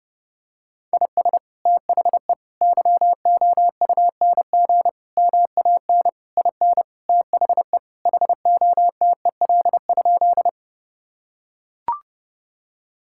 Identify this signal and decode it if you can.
Morse code, heard as a keyed tone, about 32 wpm, 700 Hz. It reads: IS THE YOUNG MAN IN THE HOTEL?
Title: Sentences from Top 1600 Words 30wpm